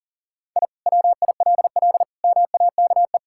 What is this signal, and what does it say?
Morse code, heard as a keyed tone, about 40 wpm, 700 Hz. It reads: I WILL MAKE
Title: Sentences from Top 1000 Words - Review Part 5 - 40wpm